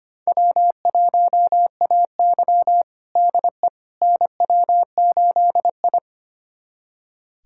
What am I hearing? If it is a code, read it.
Morse code, heard as a keyed tone, about 25 wpm, 700 Hz. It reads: W1AY DE NW8I
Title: USA Callsign DE Callsign - 4 Characters - 25wpm